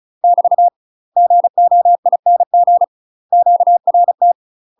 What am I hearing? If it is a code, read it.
Morse code, heard as a keyed tone, about 34 wpm, 700 Hz. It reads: <BT> GOING QRT
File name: UK MASTER QSOs TRAINING / 35WPM / UK MASTER QSOs 0004